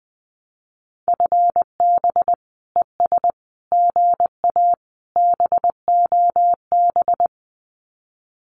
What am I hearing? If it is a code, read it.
Morse code, heard as a keyed tone, about 20 wpm, 700 Hz. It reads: FB ES GA BOB
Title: Common QSO Phrases Encore 20wpm